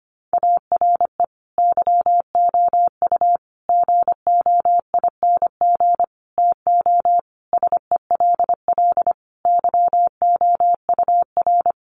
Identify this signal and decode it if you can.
Morse code, heard as a keyed tone, about 25 wpm, 700 Hz. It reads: ARE YOU GOING TO SELL YOUR
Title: Sentences from Top 1200 Words 25wpm